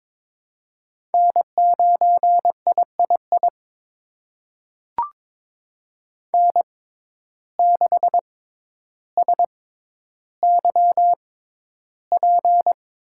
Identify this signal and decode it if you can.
Morse code, heard as a keyed tone, about 22 wpm, spaced as for 11 wpm, 700 Hz. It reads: N9III N6SYP
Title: USA Call Signs - 5 Characters - 6x Character Spacing - 22wpm